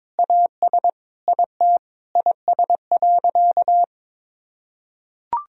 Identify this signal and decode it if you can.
Morse code, heard as a keyed tone, about 22 wpm, 700 Hz. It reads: AS IT IS.
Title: Sentences from Top 1000 Words - Review Part 3 - 22wpm